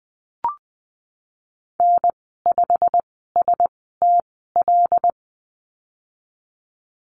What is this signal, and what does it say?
Morse code, heard as a keyed tone, about 20 wpm, spaced as for 15 wpm, 700 Hz. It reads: N5STL
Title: 128 Lesson 21g - Number 6 - Call Signs 20wpm